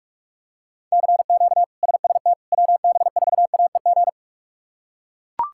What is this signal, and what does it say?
Morse code, heard as a keyed tone, about 42 wpm, 700 Hz. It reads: CQ SST WB4AEG
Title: K1USN SST QSO Practice - 45wpm